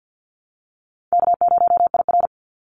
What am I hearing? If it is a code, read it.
Morse code, heard as a keyed tone, about 50 wpm, 700 Hz. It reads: K0IR
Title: Call Signs - 50wpm